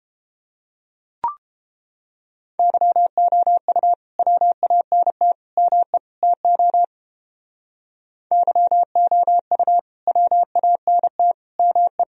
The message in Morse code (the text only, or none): YOU WANT ME TO YOU WANT ME